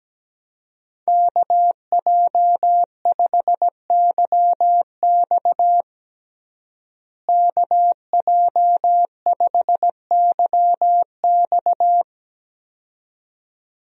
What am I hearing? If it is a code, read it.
Morse code, heard as a keyed tone, about 17 wpm, 700 Hz. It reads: KJ5YX KJ5YX
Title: USA Call Signs - 5 Characters - Repeated 2x - 17wpm